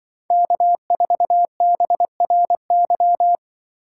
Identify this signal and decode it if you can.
Morse code, heard as a keyed tone, about 24 wpm, 700 Hz. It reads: K4BRY
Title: Speed Racing - USA Call Signs 20wpm